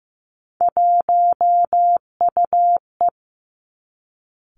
Morse code, heard as a keyed tone, about 16 wpm, 700 Hz. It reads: 1UE
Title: Three Challenging Letter-Number Combinations 15wpm